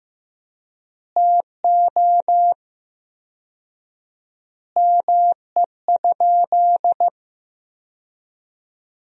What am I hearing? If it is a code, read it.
Morse code, heard as a keyed tone, about 15 wpm, 700 Hz. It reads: TO ME?
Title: 4x Word-Spacing - Sentences from Top 200 Words - 15wpm